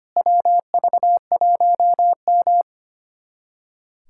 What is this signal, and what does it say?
Morse code, heard as a keyed tone, about 25 wpm, 700 Hz. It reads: WV1M
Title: USA Callsign DE Callsign - 4 Characters - 25wpm